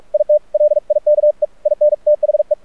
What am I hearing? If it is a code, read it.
Morse code, heard as a keyed tone, about 48 wpm, 600 Hz. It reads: IT LIKE IN THE
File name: archived / CRSnet